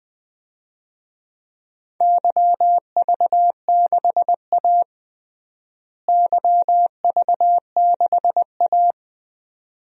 Morse code, heard as a keyed tone, about 20 wpm, 700 Hz. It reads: YV6A YV6A
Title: DXCC-100 Call Signs - 4 Characters - Repeated 2x - 20wpm